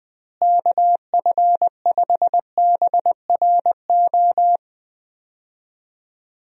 Morse code, heard as a keyed tone, about 20 wpm, 700 Hz. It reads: KF5BRO
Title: USA Call Signs - 6 Characters - Repeated 3x - 20wpm